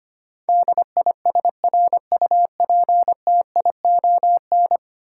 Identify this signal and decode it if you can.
Morse code, heard as a keyed tone, about 25 wpm, 690 Hz. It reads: DISRUPTION
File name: 10 Letter Words - First 3 Letters Given - 25wpm